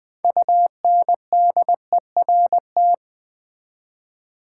Morse code, heard as a keyed tone, about 20 wpm, 700 Hz. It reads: UNDERT
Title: Build A Word - 3 Letters At A Time - 9-Letter Words - 20wpm